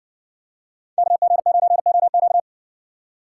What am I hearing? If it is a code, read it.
Morse code, heard as a keyed tone, about 56 wpm, 700 Hz. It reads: KG9OQ